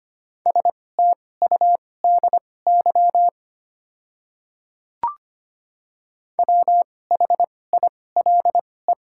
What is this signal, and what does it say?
Morse code, heard as a keyed tone, about 25 wpm, spaced as for 18 wpm, 700 Hz. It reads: STUDY WHILE
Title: 2x Character Spacing - Top 300 Words - 25wpm